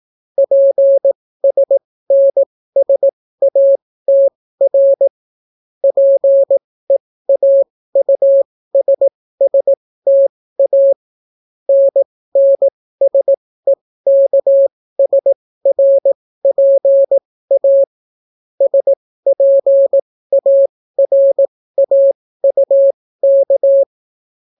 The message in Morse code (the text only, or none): PSNSATR PEAUSSTA NNSEKSRPA SPARAUK